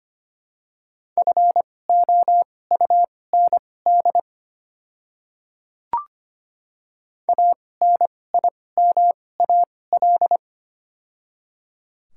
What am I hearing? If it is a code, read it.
Morse code, heard as a keyed tone, about 25 wpm, spaced as for 18 wpm, 700 Hz. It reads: FOUND ANIMAL
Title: 2x Character Spacing - Top 300 Words - 25wpm